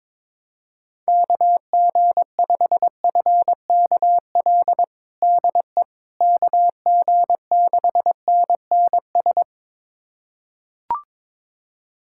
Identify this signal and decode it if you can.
Morse code, heard as a keyed tone, about 22 wpm, 700 Hz. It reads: KG5FKL DE KG6NNS
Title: USA Callsign DE Callsign 22wpm